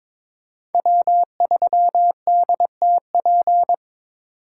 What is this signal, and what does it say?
Morse code, heard as a keyed tone, about 22 wpm, 700 Hz. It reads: W3DTP
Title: USA Call Signs - 5 Characters - 3x Character Spacing - 22wpm